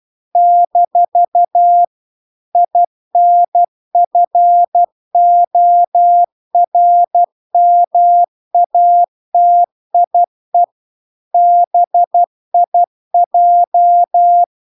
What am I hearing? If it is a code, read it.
Morse code, heard as a keyed tone, about 12 wpm, 700 Hz. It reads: - INFORMATIE BIJ
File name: cw-cursus-VERON-a03